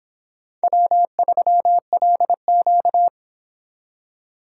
Morse code, heard as a keyed tone, about 26 wpm, 700 Hz. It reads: W3LQ